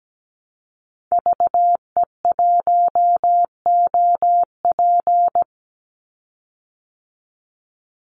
Morse code, heard as a keyed tone, about 17 wpm, 700 Hz. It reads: VE1OP
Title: Call Signs Repeated 2x - 17wpm